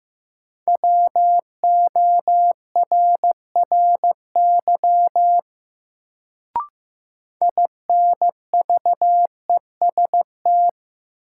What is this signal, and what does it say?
Morse code, heard as a keyed tone, about 15 wpm, 700 Hz. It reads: WORRY INVEST
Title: Sentences from Top 1100 Words - New Word Drill - 15wpm